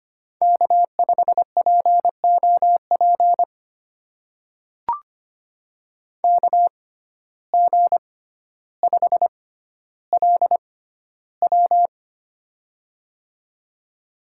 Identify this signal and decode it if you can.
Morse code, heard as a keyed tone, about 25 wpm, spaced as for 13 wpm, 700 Hz. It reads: K5POP KG5LW
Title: USA Call Signs - 6x Character Spacing - 25wpm